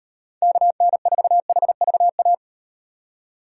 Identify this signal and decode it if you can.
Morse code, heard as a keyed tone, about 38 wpm, 700 Hz. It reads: KN4HVA